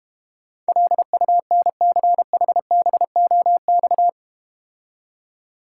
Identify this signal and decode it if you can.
Morse code, heard as a keyed tone, about 31 wpm, 700 Hz. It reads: LUNCHBOX
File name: Easy Words - Speed-Racing with Hints - 25wpm